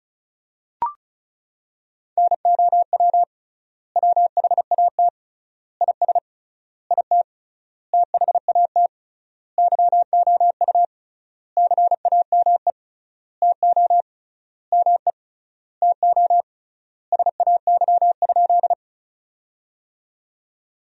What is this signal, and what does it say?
Morse code, heard as a keyed tone, about 34 wpm, 700 Hz. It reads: NOW WHAT IS IT THAT YOU CAME TO ME TO SAY?
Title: 3x Word-Spacing - Sentences from Top 200 Words - 35wpm